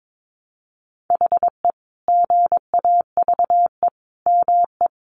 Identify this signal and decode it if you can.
Morse code, heard as a keyed tone, about 22 wpm, 700 Hz. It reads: HE GAVE ME
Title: Sentences from Top 500 Words 22wpm